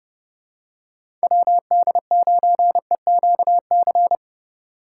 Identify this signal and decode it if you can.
Morse code, heard as a keyed tone, about 30 wpm, 700 Hz. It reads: WD9EQC
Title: USA Call Signs - 6 Characters - 4x Character Spacing - 30wpm